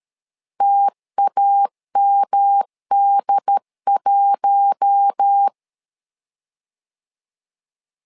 Morse code, heard as a keyed tone, about 13 wpm, 800 Hz. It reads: TAMD1
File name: LEZIONI / 4 Q1 12wpm 2028